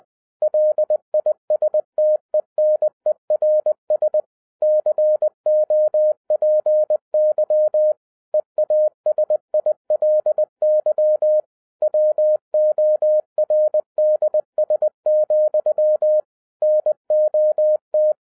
Morse code, heard as a keyed tone, about 20 wpm, 610 Hz. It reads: LISTENERS COPY EASILY WORDS, NOT
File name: 2011-10-mp3 / n13-TIPS FROM MY 34 YEARS OF EXPERIENCE 20wpm